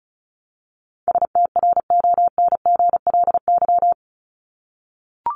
Morse code, heard as a keyed tone, about 34 wpm, 700 Hz. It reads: STRONGLY
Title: LY Suffix - Top 100 Words with Base Words 35wpm